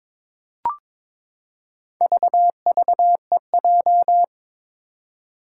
Morse code, heard as a keyed tone, about 22 wpm, 700 Hz. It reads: VVEJ
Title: Four Challenging Letter-Number Combinations 22wpm